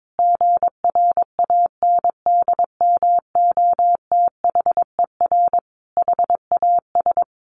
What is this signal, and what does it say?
Morse code, heard as a keyed tone, about 22 wpm, 700 Hz. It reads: GRANDMOTHER HAS
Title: Sentences from Top 1200 Words 22wpm